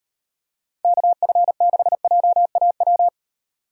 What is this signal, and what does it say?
Morse code, heard as a keyed tone, about 37 wpm, 700 Hz. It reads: KF6JAW